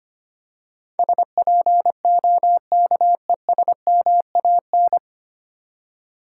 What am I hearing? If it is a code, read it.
Morse code, heard as a keyed tone, about 25 wpm, 700 Hz. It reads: SPOKESMAN
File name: practice / 25wpm / 9 Letter Words - First 3 Letters Given - 25wpm